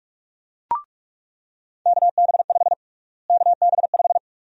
Morse code, heard as a keyed tone, about 42 wpm, 700 Hz. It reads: KB5 KB5
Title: USA Call Sign Prefixes - Repeated 2x - 45wpm